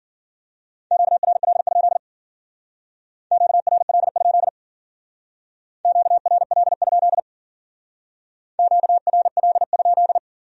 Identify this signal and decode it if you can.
Morse code, heard as a keyed tone, about 50 wpm, 700 Hz. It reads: QRL? QRL? QRL? QRL?
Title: Speed-Racing Q-Codes - 40wpm